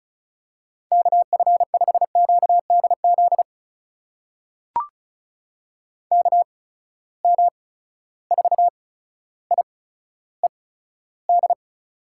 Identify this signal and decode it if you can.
Morse code, heard as a keyed tone, about 36 wpm, spaced as for 15 wpm, 700 Hz. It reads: KF5QDZ KM4IED
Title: USA Call Signs - 6 Characters - 8x Character Spacing - 35wpm